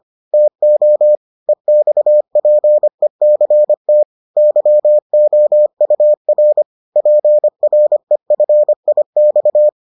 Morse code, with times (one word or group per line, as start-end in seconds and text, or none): 0.3-1.2 TO
1.5-4.0 EXPECT
4.4-6.6 YOUR
7.0-9.7 PREFIX